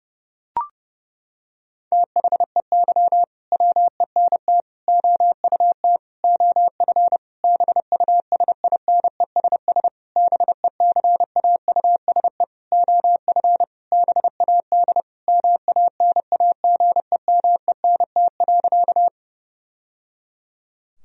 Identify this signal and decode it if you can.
Morse code, heard as a keyed tone, about 32 wpm, 700 Hz. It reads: THEY WENT OUT OF BUSINESS BECAUSE OF BAD MANAGEMENT.
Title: Sentences from Top 1000 Words - Review Part 4 - 30wpm